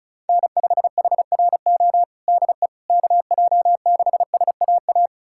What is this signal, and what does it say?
Morse code, heard as a keyed tone, about 34 wpm, 700 Hz. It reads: N5HRO DE KJ6SAA
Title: USA Callsign DE Callsign 35wpm